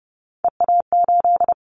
Morse code, heard as a keyed tone, about 30 wpm, 700 Hz. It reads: EA8
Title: Callsign Prefixes for Top 100 DXCC - 30wpm